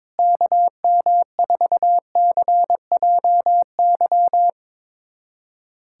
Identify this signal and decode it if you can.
Morse code, heard as a keyed tone, about 22 wpm, 700 Hz. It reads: KM4CJY
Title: USA Callsign DE Callsign - 6 Characters - 22wpm